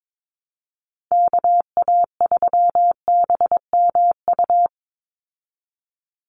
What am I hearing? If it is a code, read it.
Morse code, heard as a keyed tone, about 22 wpm, 700 Hz. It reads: KA3BMU
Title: Call Signs Repeated 2x - 22wpm